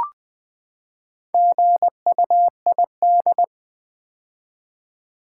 Morse code, heard as a keyed tone, about 20 wpm, 700 Hz. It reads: GUID
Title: Build A Word - 4 Letters At A Time - 8-Letter Words - 20wpm